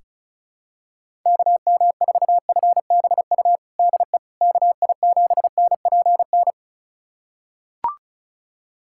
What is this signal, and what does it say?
Morse code, heard as a keyed tone, about 34 wpm, 700 Hz. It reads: KM4FBU DE KI7NPN
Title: USA Callsign DE Callsign 35wpm